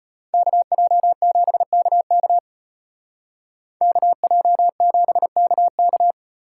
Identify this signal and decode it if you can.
Morse code, heard as a keyed tone, about 34 wpm, 700 Hz. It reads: KJ7KK KJ7KK